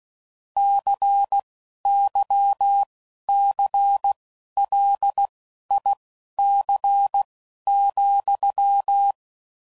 Morse code, heard as a keyed tone, about 16 wpm, spaced as for 11 wpm, 800 Hz. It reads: CYCLIC,
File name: CW Code Groups @10WPM / Technical Subjects / XL